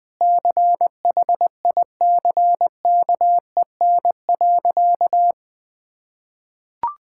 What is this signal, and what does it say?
Morse code, heard as a keyed tone, about 21 wpm, 700 Hz. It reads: CHICKEN.
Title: Sentences from Top 1000 Words - Review Part 5 - 20wpm